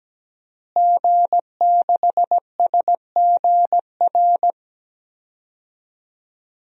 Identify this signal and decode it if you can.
Morse code, heard as a keyed tone, about 17 wpm, 700 Hz. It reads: G6SGR
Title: DXCC-100 Call Signs - 5 Characters - Repeated 3x - 17wpm